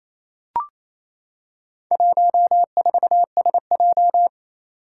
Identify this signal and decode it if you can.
Morse code, heard as a keyed tone, about 29 wpm, 700 Hz. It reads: 14SJ
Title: Four Challenging Letter-Number Combinations 28wpm